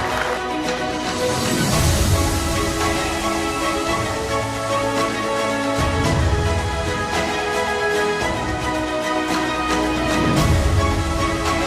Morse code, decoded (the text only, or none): none